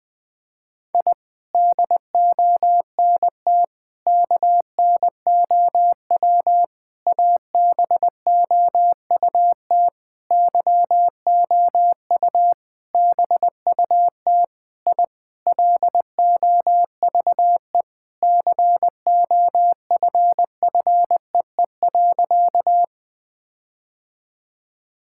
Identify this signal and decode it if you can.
Morse code, heard as a keyed tone, about 20 wpm, 700 Hz. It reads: I DONT KNOW ABOUT YOU BUT I LOVE COFFEE.
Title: Sentences with 7-Word Phrases - Spoken Beginning - 20wpm